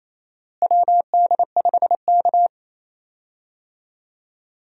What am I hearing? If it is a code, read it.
Morse code, heard as a keyed tone, about 28 wpm, 700 Hz. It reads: WD5K
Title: USA Call Signs - 4 Characters - Repeated 2x - 28wpm